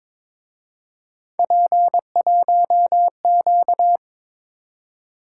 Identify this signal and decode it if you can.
Morse code, heard as a keyed tone, about 26 wpm, 700 Hz. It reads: P1Q
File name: Speed-Racing - Three Letter-Number Combinations - 22wpm